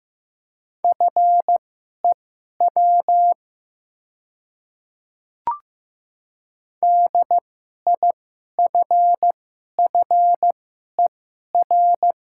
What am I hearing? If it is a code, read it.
Morse code, heard as a keyed tone, about 15 wpm, spaced as for 11 wpm, 700 Hz. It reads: FEW DIFFER
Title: 2x Character Spacing - Top 500 Words - 15wpm